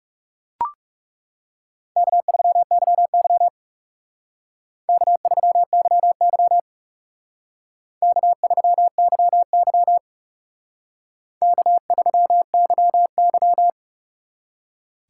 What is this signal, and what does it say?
Morse code, heard as a keyed tone, about 36 wpm, 700 Hz. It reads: K3YY K3YY K3YY K3YY
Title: Speed Racing - USA Call Signs 30wpm